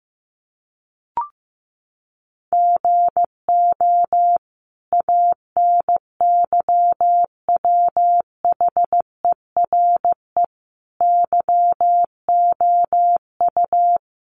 Morse code, heard as a keyed tone, about 14 wpm, 700 Hz. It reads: GO ANYWHERE YOU
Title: Sentences from Top 1400 Words 15wpm